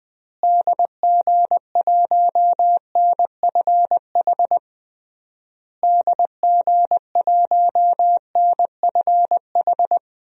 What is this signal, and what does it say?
Morse code, heard as a keyed tone, about 20 wpm, 700 Hz. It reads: DG1NFH DG1NFH
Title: DXCC-100 Call Signs - 6 Characters - Repeated 3x - 20wpm